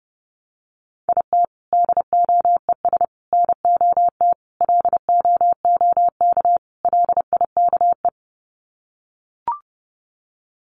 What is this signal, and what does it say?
Morse code, heard as a keyed tone, about 28 wpm, 700 Hz. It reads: IT DOES NOT LOOK LIKE
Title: Sets of 5 Words Encore2 30wpm